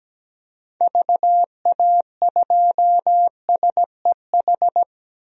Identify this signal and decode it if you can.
Morse code, heard as a keyed tone, about 17 wpm, 700 Hz. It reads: VA2SEH
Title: DXCC-100 Call Signs - 6 Characters - 17wpm